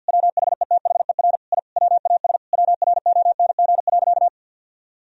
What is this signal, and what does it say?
Morse code, heard as a keyed tone, about 50 wpm, 700 Hz. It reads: WHETHER I WAS WRONG.
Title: Sentences from Top 1000 Words - Review Part 3 - 50wpm